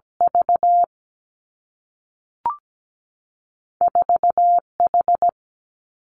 Morse code, heard as a keyed tone, about 17 wpm, 700 Hz. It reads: V 4 H
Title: Two Challenging Letter-Number Combinations 17wpm